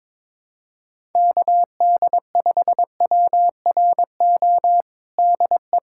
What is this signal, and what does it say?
Morse code, heard as a keyed tone, about 22 wpm, 700 Hz. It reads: KD5WRO DE
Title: USA Callsign DE Callsign - 6 Characters - 22wpm